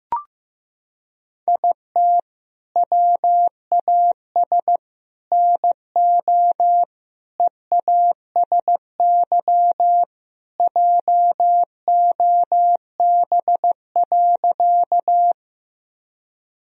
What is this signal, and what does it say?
Morse code, heard as a keyed tone, about 16 wpm, 700 Hz. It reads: IT WAS NO EASY JOB.
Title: Sentences from Top 1000 Words - Review Part 3 - 15wpm